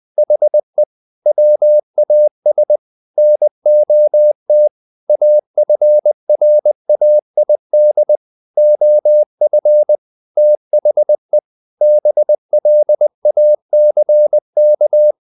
Morse code, heard as a keyed tone, about 20 wpm, 600 Hz. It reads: HE WAS NOT AFRAID OF THE BLACK
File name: the old man and the sea part 2 20wpm